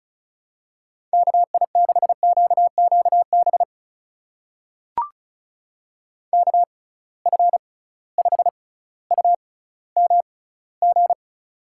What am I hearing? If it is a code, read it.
Morse code, heard as a keyed tone, about 36 wpm, spaced as for 18 wpm, 700 Hz. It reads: KI6QQB KF5UMG
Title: USA Call Signs - 6 Characters - 6x Character Spacing - 35wpm